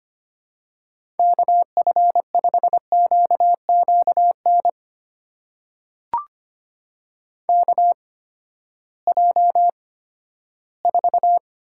Morse code, heard as a keyed tone, about 25 wpm, spaced as for 14 wpm, 700 Hz. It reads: KF5QQN KJ4
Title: USA Call Signs - 6 Characters - 8x Character Spacing - 25wpm